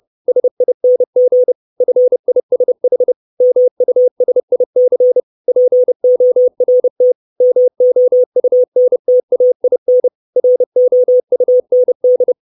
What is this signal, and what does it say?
Morse code, heard as a keyed tone, about 30 wpm, 500 Hz. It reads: SING FISH MUSIC PORT MOUNTAIN ROUND
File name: 632 Words 30WPM